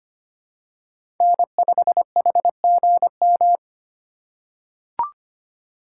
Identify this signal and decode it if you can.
Morse code, heard as a keyed tone, about 25 wpm, 700 Hz. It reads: N5HGM
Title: USA Call Signs - 5 Characters - 2x Character Spacing - 25wpm